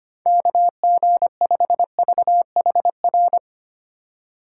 Morse code, heard as a keyed tone, about 25 wpm, 690 Hz. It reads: KG5VHR
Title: USA Call Signs - 6 Characters - 2x Character Spacing - 25wpm